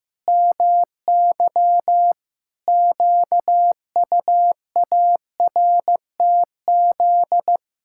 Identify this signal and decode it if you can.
Morse code, heard as a keyed tone, about 16 wpm, 700 Hz. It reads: MY QUARTZ
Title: Pangram Challenge v2 - 15wpm